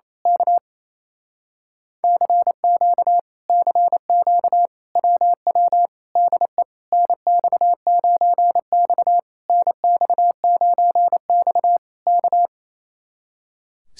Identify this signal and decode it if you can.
Morse code, heard as a keyed tone, about 27 wpm, 700 Hz. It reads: K CQ CQ WW DE NX9X NX9X K
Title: Speed-Racing Common QSO Phrases - 28wpm